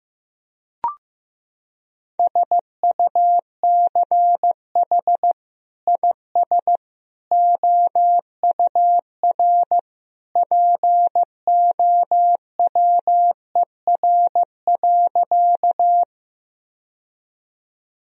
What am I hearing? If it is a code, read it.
Morse code, heard as a keyed tone, about 15 wpm, 700 Hz. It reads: SUCH IS OUR POWER.
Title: Sentences from Top 1000 Words - Review Part 2 - 15wpm